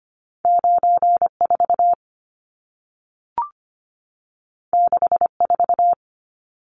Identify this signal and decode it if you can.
Morse code, heard as a keyed tone, about 25 wpm, 700 Hz. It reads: 94 64
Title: Numbers - 2-Digit 25wpm